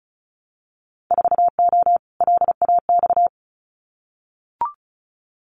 Morse code, heard as a keyed tone, about 36 wpm, 700 Hz. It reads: 4O LAX
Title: Winter Field Day Exchanges Encore 35wpm